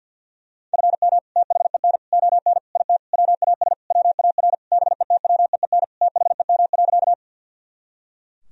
Morse code, heard as a keyed tone, about 48 wpm, 700 Hz. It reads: FM THEN ON IT WAS WAR BETWEEN THEM.
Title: Sentences from Top 400 Words and Abbreviations 50wpm